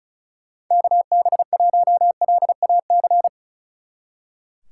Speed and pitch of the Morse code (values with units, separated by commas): 46 wpm, 700 Hz